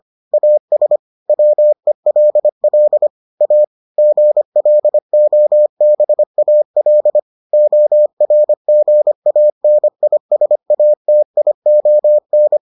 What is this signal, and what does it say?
Morse code, heard as a keyed tone, about 25 wpm, 600 Hz. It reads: AS WELL A GLOBAL ORGANISATION